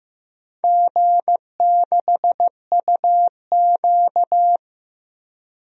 Morse code, heard as a keyed tone, about 15 wpm, 700 Hz. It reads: G6UQ
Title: DXCC-100 Call Signs - 4 Characters - Repeated 2x - 15wpm